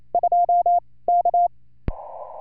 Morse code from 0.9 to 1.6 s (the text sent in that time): K